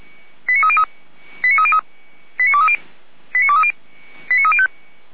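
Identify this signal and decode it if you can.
Five-tone selective call (ZVEI), heard as quick runs of five tones, tones of about 70 ms. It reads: 89303 89303 89230 89305
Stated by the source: ZVEI five-tone call 89315 received via SDR, on the Feuerwehr Orlamuende audio server.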